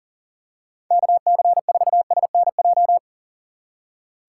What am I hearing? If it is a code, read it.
Morse code, heard as a keyed tone, about 40 wpm, 700 Hz. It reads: KC4SNJ
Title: USA Call Signs - 6 Characters - 4x Character Spacing - 40wpm